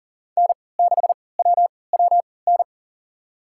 Morse code, heard as a keyed tone, about 40 wpm, spaced as for 23 wpm, 700 Hz. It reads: N6WWN
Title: USA Call Signs - 5 Characters - 3x Character Spacing - 40wpm